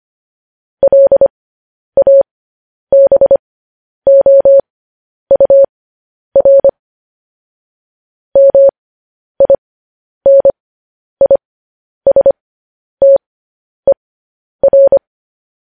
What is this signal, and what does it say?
Morse code, heard as a keyed tone, about 25 wpm, spaced as for 10 wpm, 550 Hz. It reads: LABOUR MINISTER
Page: News Headlines in Morse Code at 10 WPM